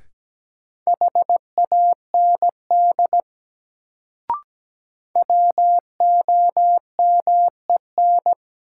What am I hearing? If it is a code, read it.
Morse code, heard as a keyed tone, about 17 wpm, 700 Hz. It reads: HAND WOMEN
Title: Sets of 3 Words - New Word Drill - 17wpm